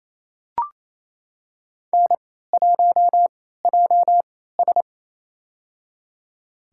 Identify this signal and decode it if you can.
Morse code, heard as a keyed tone, about 28 wpm, spaced as for 16 wpm, 700 Hz. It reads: N1JS
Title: USA Call Signs - 3x Character Spacing - 28wpm